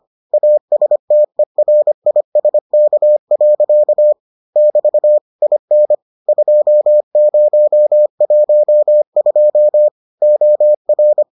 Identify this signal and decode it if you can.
Morse code, heard as a keyed tone, about 25 wpm, 600 Hz. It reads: ASTERISK. <BT> IN 2012 OR